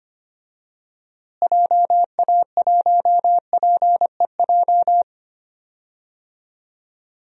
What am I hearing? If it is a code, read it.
Morse code, heard as a keyed tone, about 25 wpm, 700 Hz. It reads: JA1PEJ
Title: DXCC-100 Call Signs - 6 Characters - Repeated 3x - 25wpm